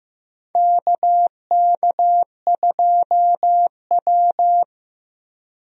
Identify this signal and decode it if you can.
Morse code, heard as a keyed tone, about 15 wpm, 700 Hz. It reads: KK2W
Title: USA Call Signs - 4 Characters - Repeated 2x - 15wpm